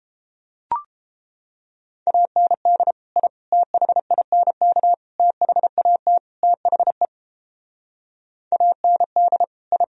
Morse code, heard as a keyed tone, about 29 wpm, 700 Hz. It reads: AND I THINK THAT THE AND I